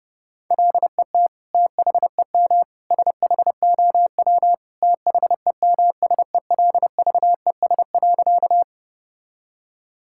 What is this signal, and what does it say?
Morse code, heard as a keyed tone, about 30 wpm, 700 Hz. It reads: LET THEM SHOW THEMSELVES.
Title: Sentences from Top 1000 Words - Review Part 5 - 30wpm